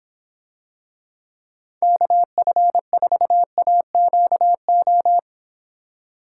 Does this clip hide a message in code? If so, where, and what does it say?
Yes: Morse code from 1.8 to 5.2 s, heard as a keyed tone, about 26 wpm, 700 Hz. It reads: KF4AQO